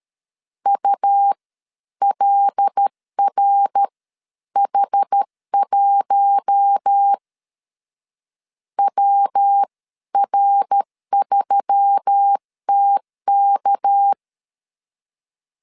Morse code, heard as a keyed tone, about 13 wpm, spaced as for 11 wpm, 800 Hz. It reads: ULRH1 WR3TK
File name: CW / DET / 1 6 1500C S C 12wpm 3533